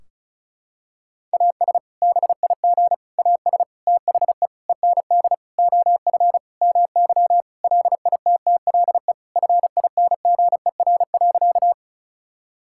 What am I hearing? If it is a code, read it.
Morse code, heard as a keyed tone, about 36 wpm, 700 Hz. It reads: AS BIG AS THE END OF MY LITTLE FINGER.
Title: Sentences from Top 1600 Words 35wpm